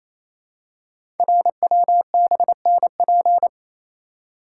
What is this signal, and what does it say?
Morse code, heard as a keyed tone, about 28 wpm, 700 Hz. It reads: RWBNP
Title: Code Groups - 5 Letters - 28wpm